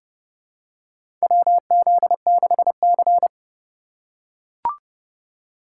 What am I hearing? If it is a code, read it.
Morse code, heard as a keyed tone, about 30 wpm, 700 Hz. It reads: WZ6C
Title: USA Call Signs - 4 Characters - 3x Character Spacing - 30wpm